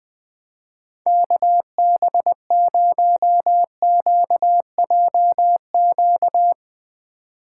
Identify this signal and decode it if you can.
Morse code, heard as a keyed tone, about 20 wpm, 700 Hz. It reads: KB0QJQ
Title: USA Call Signs - 6 Characters - 2x Character Spacing - 20wpm